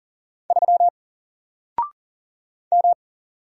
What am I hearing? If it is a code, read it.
Morse code, heard as a keyed tone, about 40 wpm, 700 Hz. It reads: 3 M
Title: Single Letter-Number Alt 40wpm